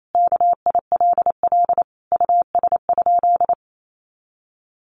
Morse code, heard as a keyed tone, about 28 wpm, 700 Hz. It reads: KILL US?
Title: Sentences from Top 1400 Words 28wpm